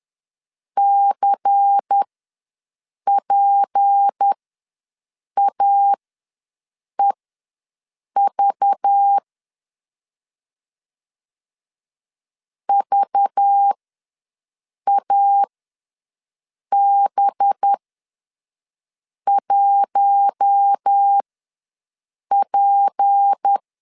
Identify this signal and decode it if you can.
Morse code, heard as a keyed tone, about 10 wpm, spaced as for 6 wpm, 800 Hz. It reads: CPAEV VAB1P
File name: LEZIONI / 6 PKX0 / 6 P ws 10wpm 3751